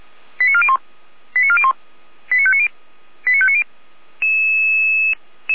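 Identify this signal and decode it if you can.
Five-tone selective call (ZVEI), heard as quick runs of five tones, tones of about 70 ms. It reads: 89401 89401 89590 89590
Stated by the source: ZVEI five-tone call received on an SDR.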